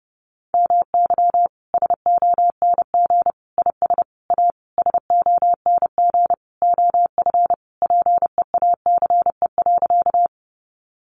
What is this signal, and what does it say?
Morse code, heard as a keyed tone, about 30 wpm, 700 Hz. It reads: MY SONG IS A SONG OF PEACE.
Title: Sentences from Top 1500 Words 30wpm